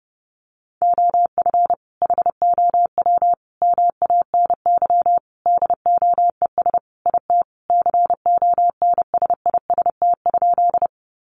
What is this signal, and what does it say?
Morse code, heard as a keyed tone, about 30 wpm, 700 Hz. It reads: OF HOW MANY DOES IT CONSIST?
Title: Sentences from Top 1400 Words 30wpm